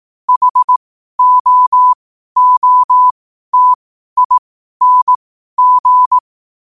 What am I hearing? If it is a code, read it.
Morse code, heard as a keyed tone, about 18 wpm, spaced as for 12 wpm, 1000 Hz. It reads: HOOTING
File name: qotdfiles / 12wpm